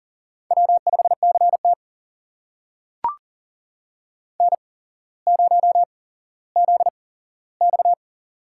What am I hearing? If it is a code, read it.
Morse code, heard as a keyed tone, about 40 wpm, spaced as for 17 wpm, 700 Hz. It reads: W5CT N0ZX
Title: USA Call Signs - 4 Characters - 8x Character Spacing - 40wpm